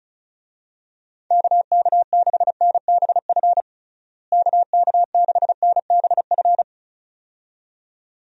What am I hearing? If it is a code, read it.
Morse code, heard as a keyed tone, about 34 wpm, 700 Hz. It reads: KK6NBF KK6NBF
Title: USA Call Signs - 6 Characters - Repeated 2x - 35wpm